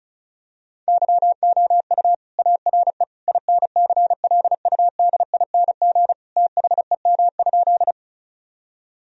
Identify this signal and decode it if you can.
Morse code, heard as a keyed tone, about 34 wpm, 700 Hz. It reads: YOU ARE INCLUDING THEM?
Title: Sentences from Top 1000 Words - Review Part 4 - 35wpm